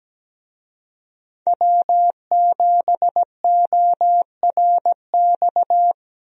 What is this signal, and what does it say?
Morse code, heard as a keyed tone, about 17 wpm, 700 Hz. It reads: W7ORX
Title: USA Call Signs - 5 Characters - Repeated 2x - 17wpm